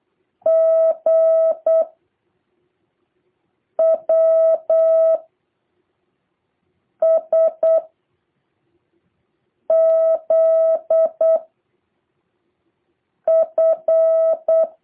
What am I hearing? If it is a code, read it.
Morse code, heard as a keyed tone, about 8 wpm, 650 Hz. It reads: G W S Z F
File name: jh8cbh / 2023cw